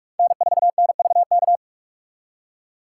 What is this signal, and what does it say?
Morse code, heard as a keyed tone, about 57 wpm, 700 Hz. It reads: N4NVK